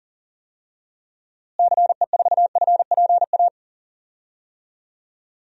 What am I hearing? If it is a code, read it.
Morse code, heard as a keyed tone, about 40 wpm, 700 Hz. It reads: CE4FPA
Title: DXCC-100 Call Signs - 6 Characters - Repeated 2x - 40wpm